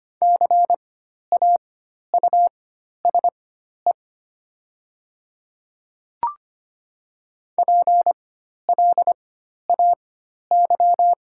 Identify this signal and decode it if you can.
Morse code, heard as a keyed tone, about 25 wpm, spaced as for 12 wpm, 700 Hz. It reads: CAUSE PLAY
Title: 4x Character Spacing - Top 500 Words - 25wpm